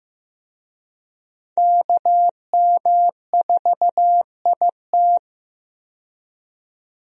Morse code, heard as a keyed tone, about 15 wpm, 700 Hz. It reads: KM4IT
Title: USA Call Signs - 5 Characters - 15wpm